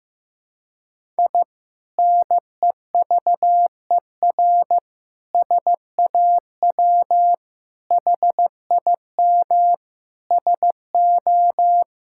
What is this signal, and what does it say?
Morse code, heard as a keyed tone, about 15 wpm, 700 Hz. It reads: I NEVER SAW HIM SO
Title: Sentences from Top 1000 Words - Review Part 6 - 15wpm